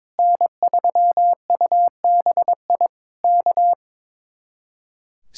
Morse code, heard as a keyed tone, about 22 wpm, 700 Hz. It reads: N3UBI K
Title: Speed-Racing Common QSO Phrases - 22wpm